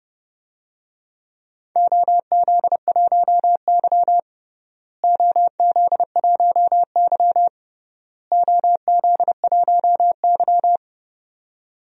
Morse code, heard as a keyed tone, about 31 wpm, 700 Hz. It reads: OZ1Y OZ1Y OZ1Y
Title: DXCC-100 Call Signs - 4 Characters - Repeated 3x - 30wpm